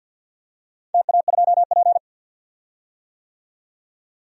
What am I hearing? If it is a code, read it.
Morse code, heard as a keyed tone, about 51 wpm, 700 Hz. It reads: TA2P
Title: DXCC-100 Call Signs - 4 Characters - 50wpm